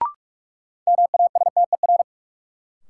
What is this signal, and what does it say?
Morse code, heard as a keyed tone, about 46 wpm, 700 Hz. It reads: MASTER
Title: Sentences from Top 1300 Words - New Word Drill - 45wpm